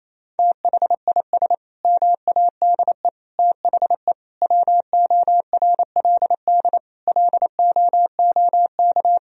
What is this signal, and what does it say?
Morse code, heard as a keyed tone, about 28 wpm, 700 Hz. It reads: THIS MADE THE WORLD LOOK
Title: Sentences from Top 1000 Words - Review Part 2 - 28wpm